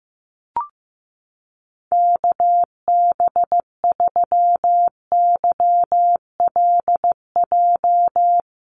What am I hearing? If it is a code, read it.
Morse code, heard as a keyed tone, about 15 wpm, 700 Hz. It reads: KB3YLJ
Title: USA Call Signs Repeated 3x 15wpm